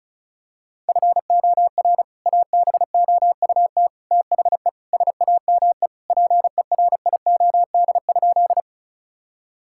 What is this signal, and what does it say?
Morse code, heard as a keyed tone, about 36 wpm, 700 Hz. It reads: FOR ABOUT THE SAME PERIOD?
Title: Sentences from Top 1000 Words - Review Part 5 - 35wpm